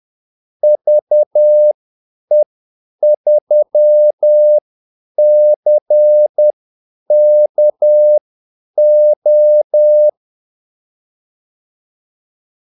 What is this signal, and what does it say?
Morse code, heard as a keyed tone, about 10 wpm, spaced as for 8 wpm, 600 Hz. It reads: VE3CKO
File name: Callsigns-10-8wpm